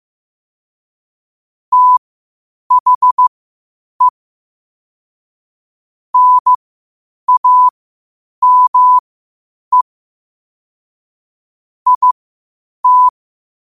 Morse code, heard as a keyed tone, about 15 wpm, spaced as for 8 wpm, 1000 Hz. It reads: THE NAME IT
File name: qotdfiles / 8wpm